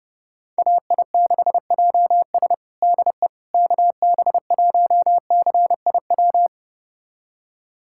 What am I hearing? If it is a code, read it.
Morse code, heard as a keyed tone, about 30 wpm, 700 Hz. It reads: AI6JS DE KB1CIW
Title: USA Callsign DE Callsign 30wpm